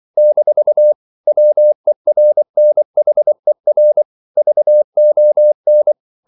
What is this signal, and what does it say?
Morse code, heard as a keyed tone, about 24 wpm, 600 Hz. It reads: - WERNHER VON
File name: CW / Practice / EnglishQuotations0424